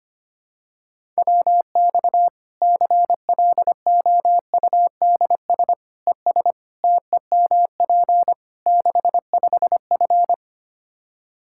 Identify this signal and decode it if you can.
Morse code, heard as a keyed tone, about 25 wpm, 700 Hz. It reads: WX CLOUDS ES TEMP 65F
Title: Practice QSO - Exchange 2 - Info - 25wpm